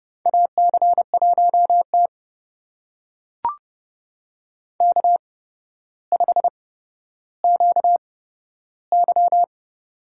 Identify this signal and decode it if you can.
Morse code, heard as a keyed tone, about 30 wpm, spaced as for 13 wpm, 700 Hz. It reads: AC1T K5QY
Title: USA Call Signs - 4 Characters - 8x Character Spacing - 30wpm